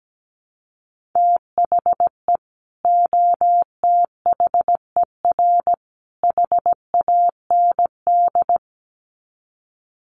Morse code, heard as a keyed tone, about 17 wpm, 700 Hz. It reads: THE OTHER HAND
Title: Sets of 3 Words 17wpm